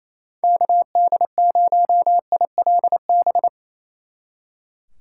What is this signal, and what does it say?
Morse code, heard as a keyed tone, about 35 wpm, 700 Hz. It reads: KD0ILB
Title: Speed Racing - USA 2x3 Call Signs 28wpm